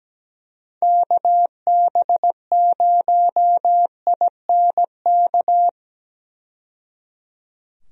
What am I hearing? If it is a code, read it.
Morse code, heard as a keyed tone, about 17 wpm, 700 Hz. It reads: KB0INK
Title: USA Call Signs - 6 Characters - 17wpm